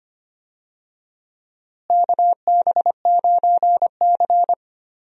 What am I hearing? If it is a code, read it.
Morse code, heard as a keyed tone, about 25 wpm, 700 Hz. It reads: KB9C
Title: USA Call Signs - 4 Characters - Repeated 3x - 25wpm